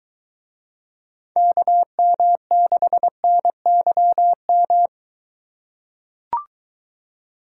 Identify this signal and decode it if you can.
Morse code, heard as a keyed tone, about 22 wpm, 700 Hz. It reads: KM6NYM